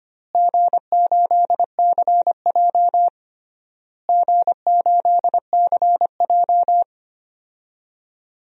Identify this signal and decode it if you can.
Morse code, heard as a keyed tone, about 25 wpm, 700 Hz. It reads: G8CJ G8CJ
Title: DXCC-100 Call Signs - 4 Characters - Repeated 3x - 25wpm